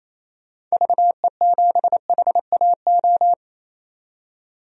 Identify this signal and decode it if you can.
Morse code, heard as a keyed tone, about 28 wpm, 700 Hz. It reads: VE7HAO
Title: DXCC-100 Call Signs - 6 Characters - 28wpm